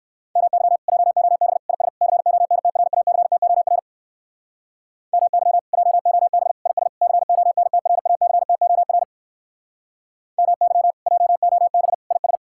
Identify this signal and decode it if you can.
Morse code, heard as a keyed tone, about 59 wpm, 700 Hz. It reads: MY JOB IS CONTRACTOR MY JOB IS CONTRACTOR MY JOB IS